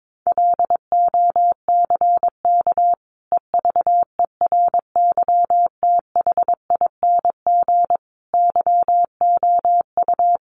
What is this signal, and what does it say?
Morse code, heard as a keyed tone, about 22 wpm, 700 Hz. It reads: LOCK EVERYTHING YOU
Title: Sentences from Top 1300 Words 22wpm